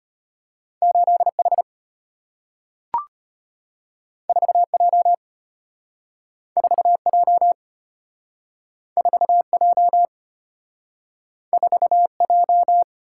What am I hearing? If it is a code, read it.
Morse code, heard as a keyed tone, about 33 wpm, 700 Hz. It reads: 8H 4J 4J 4J 4J